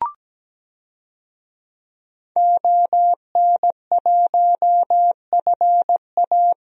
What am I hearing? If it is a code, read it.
Morse code, heard as a keyed tone, about 17 wpm, 700 Hz. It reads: ON1FA
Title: DXCC-100 Call Signs - 5 Characters - Repeated 2x - 17wpm